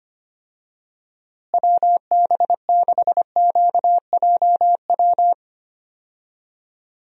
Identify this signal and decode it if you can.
Morse code, heard as a keyed tone, about 25 wpm, 700 Hz. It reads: WB6QJW
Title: USA Call Signs - 6 Characters - Repeated 3x - 25wpm